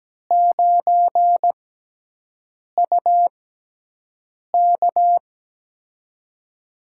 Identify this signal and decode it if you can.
Morse code, heard as a keyed tone, about 17 wpm, spaced as for 6 wpm, 700 Hz. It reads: 9UK
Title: USA Call Signs - 5 Characters - 6x Character Spacing - 17wpm